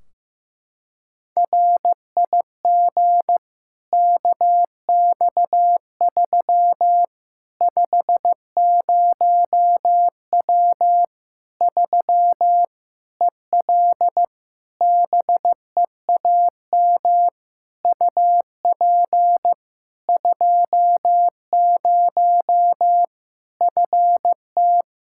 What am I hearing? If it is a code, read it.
Morse code, heard as a keyed tone, about 15 wpm, 700 Hz. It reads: RIG KX3 50W 3 EL BEAM UP 20 FT
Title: Practice QSO - Exchange 2 - Info - 15wpm